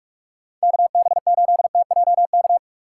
Morse code, heard as a keyed tone, about 46 wpm, 700 Hz. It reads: KB8TJK
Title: USA Call Signs - 6 Characters - 2x Character Spacing - 45wpm